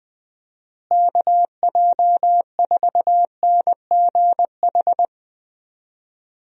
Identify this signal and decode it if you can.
Morse code, heard as a keyed tone, about 20 wpm, 700 Hz. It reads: KJ4NGH
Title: USA Call Signs - 6 Characters - Repeated 2x - 20wpm